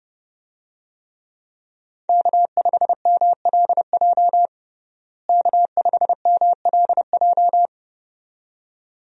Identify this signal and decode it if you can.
Morse code, heard as a keyed tone, about 32 wpm, 700 Hz. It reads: K5MLJ K5MLJ
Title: USA Call Signs - 5 Characters - Repeated 2x - 30wpm